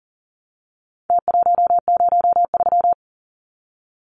Morse code, heard as a keyed tone, about 40 wpm, 700 Hz. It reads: T103
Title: Numbers - Contest Style - 4-Digit 40wpm